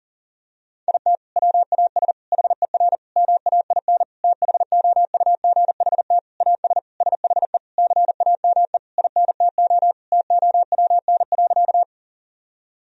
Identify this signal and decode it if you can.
Morse code, heard as a keyed tone, about 40 wpm, 700 Hz. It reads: IT WAS HER MAIN THOUGHT AS SHE CAME INTO TOWN.
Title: Sentences from Top 1000 Words - Review Part 2 - 40wpm